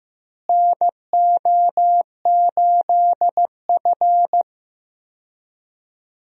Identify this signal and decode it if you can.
Morse code, heard as a keyed tone, about 15 wpm, 700 Hz. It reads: NO8F
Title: USA Call Signs - 4 Characters - Repeated 3x - 15wpm